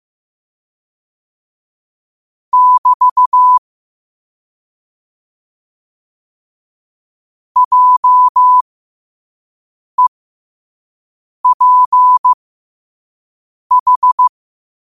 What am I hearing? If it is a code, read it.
Morse code, heard as a keyed tone, about 15 wpm, spaced as for 5 wpm, 1000 Hz. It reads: <BT> JEPH
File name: qotdfiles / 5wpm